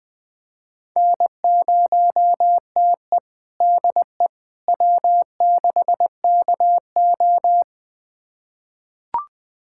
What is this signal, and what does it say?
Morse code, heard as a keyed tone, about 20 wpm, 700 Hz. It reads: N0TE DE W6KO
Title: USA Callsign DE Callsign - 4 Characters - 20wpm